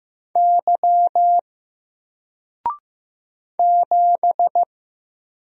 Morse code, heard as a keyed tone, about 15 wpm, 700 Hz. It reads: Y 7
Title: Single Letter-Number Alt 15wpm